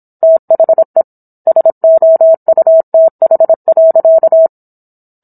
Morse code, heard as a keyed tone, about 27 wpm, 650 Hz. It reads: THE SOUTH.